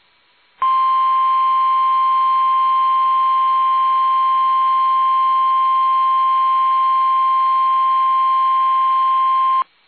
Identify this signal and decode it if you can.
Emergency Alert System (SAME) tones (the 1050 Hz attention tone). It is heard from 0.6 to 9.6 s.
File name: eas / KUSO / KUSO